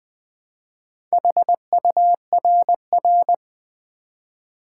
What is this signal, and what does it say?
Morse code, heard as a keyed tone, about 20 wpm, 700 Hz. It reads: HURR
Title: Build A Word - 1 Letter At A Time - 5-Letter Words - 20wpm